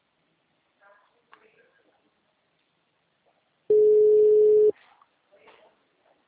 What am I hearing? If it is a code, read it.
Ringback tone (Europe, 425 Hz), with speech.
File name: out- -2020- - - .44089